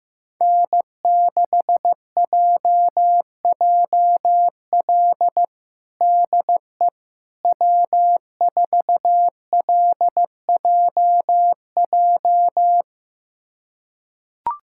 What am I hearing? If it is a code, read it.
Morse code, heard as a keyed tone, about 15 wpm, 700 Hz. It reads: N6JJL DE W4LJJ
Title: USA Callsign DE Callsign - 5 Characters - 15wpm